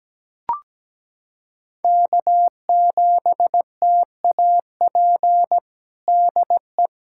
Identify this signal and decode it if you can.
Morse code, heard as a keyed tone, about 17 wpm, 700 Hz. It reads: K7TAP DE
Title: USA Callsign DE Callsign 17wpm